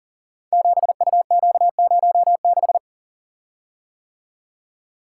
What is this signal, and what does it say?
Morse code, heard as a keyed tone, about 38 wpm, 690 Hz. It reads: 7UQ06
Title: Code Groups - 5 Letter:Numbers - 40wpm